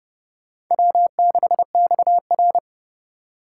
Morse code, heard as a keyed tone, about 30 wpm, 700 Hz. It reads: W6XR
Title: K1USN SST QSO Practice - 30wpm